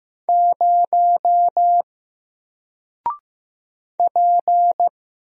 Morse code, heard as a keyed tone, about 15 wpm, 700 Hz. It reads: 0 P
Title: Single Character (Letter:Number:Punctuation:Prosign) Alt 15wpm